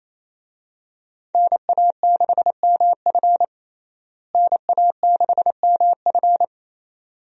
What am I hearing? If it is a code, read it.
Morse code, heard as a keyed tone, about 27 wpm, 700 Hz. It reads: NA6MF NA6MF
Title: USA Call Signs - 5 Characters - Repeated 2x - 28wpm